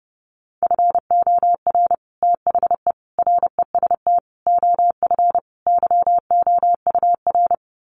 Morse code, heard as a keyed tone, about 30 wpm, 700 Hz. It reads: FOR THE REST OF YOUR
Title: Sets of 5 Words Encore 30wpm